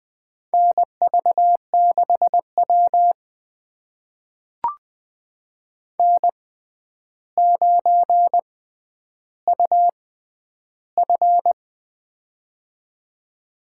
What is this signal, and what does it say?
Morse code, heard as a keyed tone, about 20 wpm, spaced as for 10 wpm, 700 Hz. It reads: NV6W N9UF
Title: USA Call Signs - 4 Characters - 6x Character Spacing - 20wpm